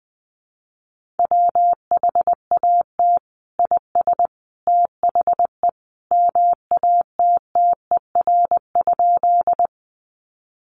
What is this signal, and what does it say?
Morse code, heard as a keyed tone, about 20 wpm, 700 Hz. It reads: WHAT IS THE MATTER?
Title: Sentences from Top 700 Words 20wpm